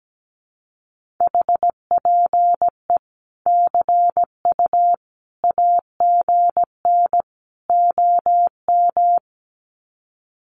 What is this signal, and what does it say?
Morse code, heard as a keyed tone, about 17 wpm, 700 Hz. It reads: HPE CU AGN OM
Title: Common QSO Phrases 17wpm